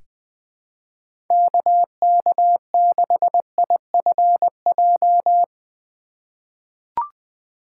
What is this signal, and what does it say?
Morse code, heard as a keyed tone, about 21 wpm, 700 Hz. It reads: KK6IFJ
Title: USA Call Signs - 6 Characters - 2x Character Spacing - 20wpm